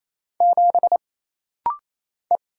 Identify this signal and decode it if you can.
Morse code, heard as a keyed tone, about 27 wpm, 700 Hz. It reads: 7 E
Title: Single Letter-Number - Rapid-Fire 28wpm